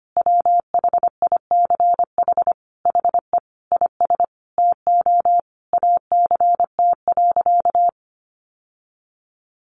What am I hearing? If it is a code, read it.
Morse code, heard as a keyed tone, about 25 wpm, 700 Hz. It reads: WHICH HE IS TO ACT.
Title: Sentences from Top 1200 Words 25wpm